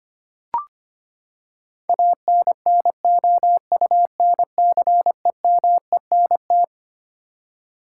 Morse code, heard as a keyed tone, about 25 wpm, 700 Hz. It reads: ANNOUNCEMENT
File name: Twelve Letter Words - 25wpm